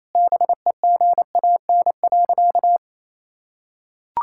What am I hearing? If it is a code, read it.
Morse code, heard as a keyed tone, about 28 wpm, 700 Hz. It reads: BEGAN.
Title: Sentences from Top 1000 Words - Review Part 5 - 28wpm